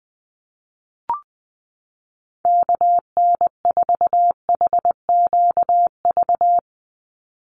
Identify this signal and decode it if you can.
Morse code, heard as a keyed tone, about 20 wpm, 700 Hz. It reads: KN4HQV
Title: USA Call Signs 20wpm